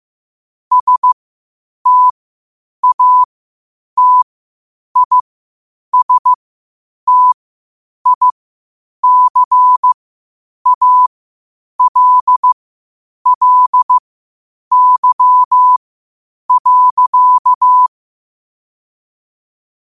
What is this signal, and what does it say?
Morse code, heard as a keyed tone, about 15 wpm, spaced as for 8 wpm, 1000 Hz. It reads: STATISTICALLY.
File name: qotdfiles / 8wpm